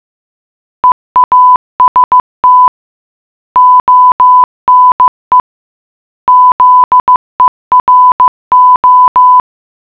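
Morse code, heard as a keyed tone, about 15 wpm, 1000 Hz. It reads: EAST ONE ZERO
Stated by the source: Morse puzzle for the geocache Lambertsetersenderen - Gacrux#8 (GCRHX0) in Oslo, Norway.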